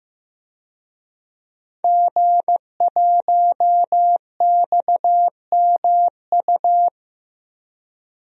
Morse code, heard as a keyed tone, about 15 wpm, 700 Hz. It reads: G1XMU
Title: DXCC-100 Call Signs - 5 Characters - 15wpm